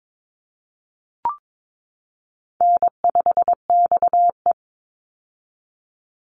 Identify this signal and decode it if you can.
Morse code, heard as a keyed tone, about 22 wpm, 700 Hz. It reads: N5XE
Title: Easy Call Signs - 22wpm